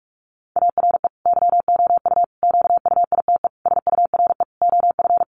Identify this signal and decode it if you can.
Morse code, heard as a keyed tone, about 44 wpm, 700 Hz. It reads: ARE YOU QUITE SURE OF
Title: Sentences from Top 700 Words 45wpm